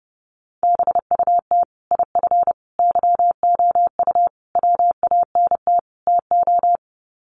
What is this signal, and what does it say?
Morse code, heard as a keyed tone, about 30 wpm, 700 Hz. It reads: BUT IF YOU WANT TO
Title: Sets of 5 Words Encore2 30wpm